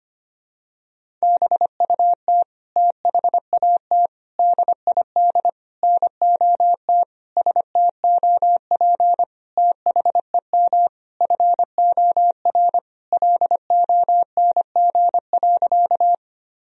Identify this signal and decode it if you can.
Morse code, heard as a keyed tone, about 25 wpm, 700 Hz. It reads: BUT THAT DID NOT STOP THEM FOR LONG.
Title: Sentences from Top 1000 Words - Review Part 1 - 25wpm